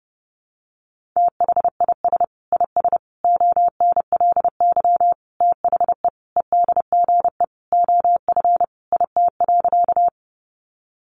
Morse code, heard as a keyed tone, about 28 wpm, 700 Hz. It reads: THIS IS ONLY THE EDGE OF IT.
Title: Sentences from Top 1300 Words 30wpm